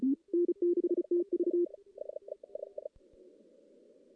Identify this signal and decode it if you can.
Morse code, heard as a keyed tone, about 34 wpm, 350 Hz. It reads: N6TV
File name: contests / CW-SS-k5tr- -1600